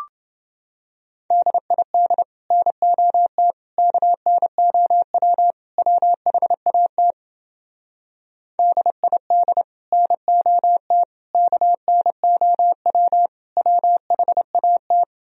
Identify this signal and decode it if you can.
Morse code, heard as a keyed tone, about 30 wpm, 700 Hz. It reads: DID NOT KNOW WHAT DID NOT KNOW WHAT